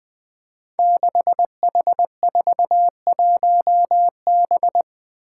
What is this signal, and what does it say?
Morse code, heard as a keyed tone, about 20 wpm, 700 Hz. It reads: 6H41B
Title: Five Challenging Letter-Number Combinations 20wpm